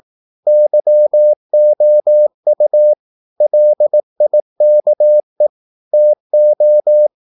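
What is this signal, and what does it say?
Morse code, heard as a keyed tone, about 18 wpm, 600 Hz. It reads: YOU LIKE TO
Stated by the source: Morse practice at 18 wpm, English text, from CW med Gnister.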